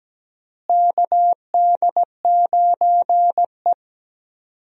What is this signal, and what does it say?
Morse code, heard as a keyed tone, about 17 wpm, 700 Hz. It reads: KD9E
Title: USA Call Signs - 4 Characters - Repeated 3x - 17wpm